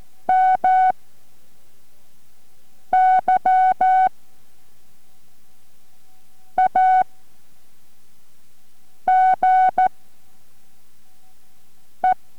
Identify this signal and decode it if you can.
Morse code, heard as a keyed tone, about 14 wpm, spaced as for 4 wpm, 750 Hz. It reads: MY AGE